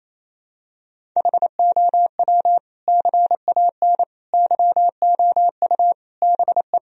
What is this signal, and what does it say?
Morse code, heard as a keyed tone, about 28 wpm, 700 Hz. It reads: HOW CAN YOU BE
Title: Sentences from Top 1000 Words - Review Part 1 - 28wpm